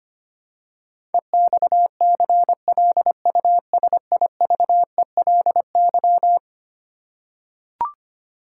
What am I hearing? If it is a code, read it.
Morse code, heard as a keyed tone, about 25 wpm, 700 Hz. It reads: EXCLUSIVELY
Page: morse-code-ninja-practice-25wpm